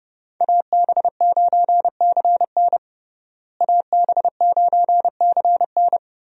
Morse code, heard as a keyed tone, about 32 wpm, 700 Hz. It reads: AB9CN AB9CN
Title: USA Call Signs - 5 Characters - Repeated 3x - 30wpm